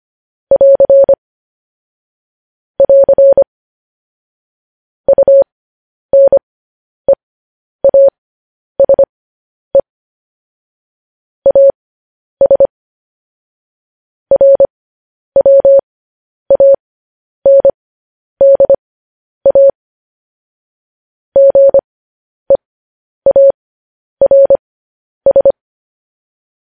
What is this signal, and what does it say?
Morse code, heard as a keyed tone, about 25 wpm, spaced as for 10 wpm, 550 Hz. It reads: <AR> <AR> UNEASE AS RWANDA GEARS
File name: podcast / News headlines-1x-25x10